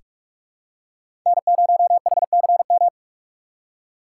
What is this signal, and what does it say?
Morse code, heard as a keyed tone, about 47 wpm, 700 Hz. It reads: N0HCM
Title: USA Call Signs - 5 Characters - 6x Character Spacing - 45wpm